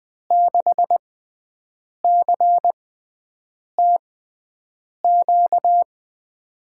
Morse code, heard as a keyed tone, about 20 wpm, spaced as for 7 wpm, 700 Hz. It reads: 6CTQ
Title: USA Call Signs - 5 Characters - 6x Character Spacing - 20wpm